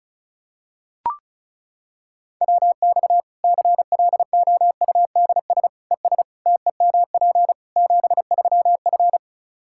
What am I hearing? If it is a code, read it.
Morse code, heard as a keyed tone, about 34 wpm, 700 Hz. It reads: WX CLOUDS ES TEMP 73F
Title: Practice QSO - Exchange 2 - Info - 35wpm